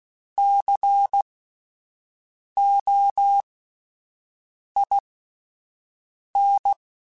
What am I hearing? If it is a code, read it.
Morse code, heard as a keyed tone, about 16 wpm, spaced as for 5 wpm, 800 Hz. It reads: COIN
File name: Alphabetical / Alphabetical mp3s / Cs @5